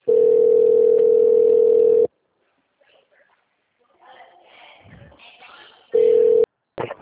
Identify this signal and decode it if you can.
Ringback tone (North America), 440 + 480 Hz, with speech.
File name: out- -2023- - - .93